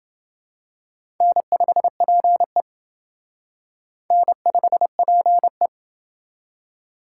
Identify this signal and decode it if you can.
Morse code, heard as a keyed tone, about 28 wpm, 700 Hz. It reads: N5PE N5PE